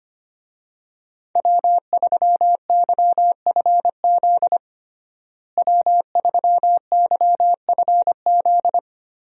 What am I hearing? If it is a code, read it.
Morse code, heard as a keyed tone, about 25 wpm, 700 Hz. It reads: W3YFZ W3YFZ
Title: USA Call Signs - 5 Characters - Repeated 3x - 25wpm